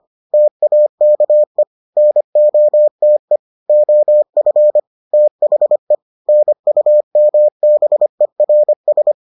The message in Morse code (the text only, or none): TAKE NOTE OF THE NUMBERS